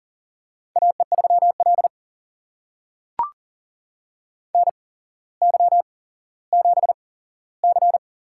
Morse code, heard as a keyed tone, about 42 wpm, spaced as for 17 wpm, 700 Hz. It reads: AE3L NY7C
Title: USA Call Signs - 4 Characters - 8x Character Spacing - 40wpm